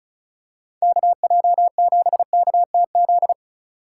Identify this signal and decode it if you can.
Morse code, heard as a keyed tone, about 34 wpm, 700 Hz. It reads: KJ7KTZ
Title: USA Call Signs - 6 Characters - 2x Character Spacing - 35wpm